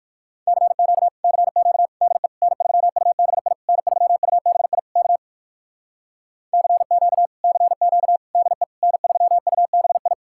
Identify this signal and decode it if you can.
Morse code, heard as a keyed tone, about 56 wpm, 700 Hz. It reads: CQ CQ DE N3UBI N3UBI K CQ CQ DE N3UBI
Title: Speed-Racing Common QSO Phrases - 45wpm